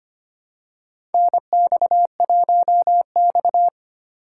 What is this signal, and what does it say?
Morse code, heard as a keyed tone, about 25 wpm, 700 Hz. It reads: NX1X
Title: USA Call Signs - 4 Characters - 6x Character Spacing - 25wpm